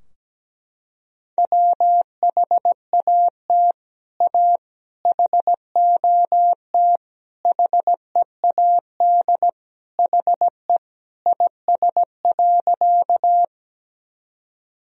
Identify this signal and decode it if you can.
Morse code, heard as a keyed tone, about 17 wpm, 700 Hz. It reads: WHAT A HOT HEAD HE IS.
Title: Sentences from Top 1000 Words - Review Part 3 - 17wpm